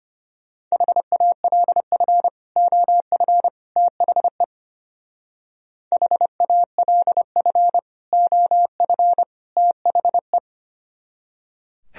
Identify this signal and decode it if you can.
Morse code, heard as a keyed tone, about 30 wpm, 700 Hz. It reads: HALF OF THE HALF OF THE
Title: Speed-Racing - Sets of 3 Words 25wpm